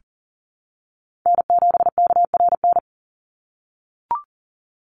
Morse code, heard as a keyed tone, about 40 wpm, 700 Hz. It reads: N7KRN
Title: NA SOTA Chaser Callsigns 40wpm